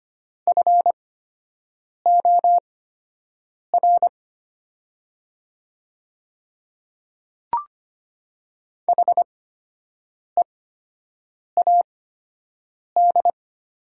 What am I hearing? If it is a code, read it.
Morse code, heard as a keyed tone, about 25 wpm, spaced as for 7 wpm, 700 Hz. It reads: FOR HEAD
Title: 8x Character Spacing - Top 300 Words - 25wpm